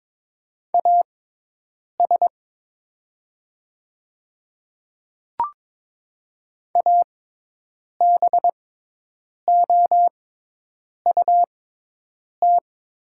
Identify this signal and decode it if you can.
Morse code, heard as a keyed tone, about 22 wpm, spaced as for 8 wpm, 690 Hz. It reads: AS ABOUT
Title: 6x Character Spacing - Top 100 Words - 22wpm